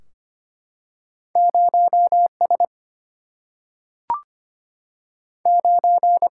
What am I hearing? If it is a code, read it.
Morse code, heard as a keyed tone, about 25 wpm, 700 Hz. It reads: 0 S 9
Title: Single-Letter Single-Number 25wpm